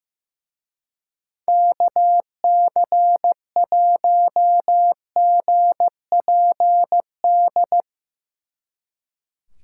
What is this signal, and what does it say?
Morse code, heard as a keyed tone, about 17 wpm, 700 Hz. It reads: KC1GPD
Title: Speed Racing - USA 2x3 Call Signs 15wpm